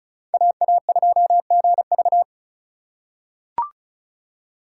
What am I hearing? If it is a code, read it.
Morse code, heard as a keyed tone, about 36 wpm, 700 Hz. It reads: AA2GV
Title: USA Call Signs - 5 Characters - 3x Character Spacing - 35wpm